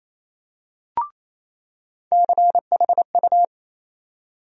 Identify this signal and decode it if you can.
Morse code, heard as a keyed tone, about 28 wpm, 700 Hz. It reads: CHU
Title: USA Call Sign Postfixes - 3 Letters 28wpm